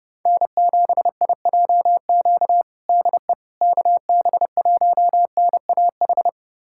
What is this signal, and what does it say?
Morse code, heard as a keyed tone, about 32 wpm, 700 Hz. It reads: N7IJQ DE KB1NAH
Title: USA Callsign DE Callsign 30wpm